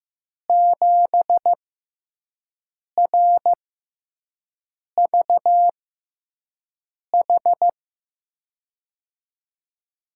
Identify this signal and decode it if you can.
Morse code, heard as a keyed tone, about 15 wpm, spaced as for 5 wpm, 700 Hz. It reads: 7RVH
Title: USA Call Signs - 6 Characters - 6x Character Spacing - 15wpm